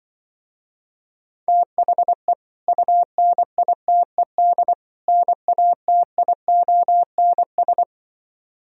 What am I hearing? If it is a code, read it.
Morse code, heard as a keyed tone, about 24 wpm, 700 Hz. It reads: THE UNITED NATIONS
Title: Speed-Racing - Sets of 3 Words 20wpm